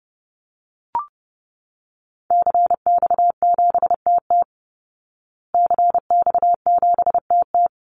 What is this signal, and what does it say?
Morse code, heard as a keyed tone, about 30 wpm, 700 Hz. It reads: CX7TT CX7TT
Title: Call Signs Repeated 3x - 30wpm